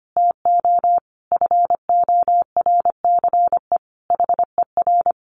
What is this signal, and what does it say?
Morse code, heard as a keyed tone, about 25 wpm, 700 Hz. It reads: TO FORCE HER
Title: Sentences from Top 1500 Words 25wpm